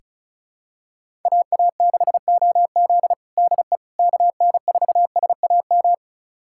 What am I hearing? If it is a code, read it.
Morse code, heard as a keyed tone, about 36 wpm, 700 Hz. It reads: AA6OZ DE KN4SAM
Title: USA Callsign DE Callsign 35wpm